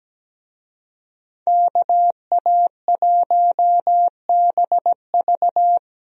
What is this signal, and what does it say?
Morse code, heard as a keyed tone, about 17 wpm, 700 Hz. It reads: KA1BV
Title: USA Call Signs - 5 Characters - 17wpm